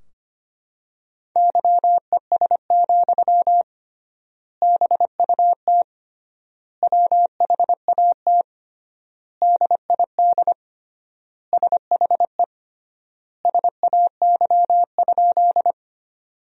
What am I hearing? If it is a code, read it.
Morse code, heard as a keyed tone, about 25 wpm, 700 Hz. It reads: YES, BUT WHAT DID SHE SAY?
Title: 3x Word-Spacing - Sentences from Top 500 Words - 25wpm